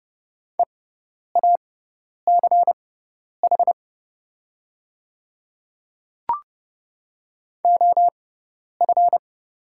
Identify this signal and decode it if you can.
Morse code, heard as a keyed tone, about 30 wpm, spaced as for 10 wpm, 700 Hz. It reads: EACH OF
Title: 6x Character Spacing - Top 200 Words - 30wpm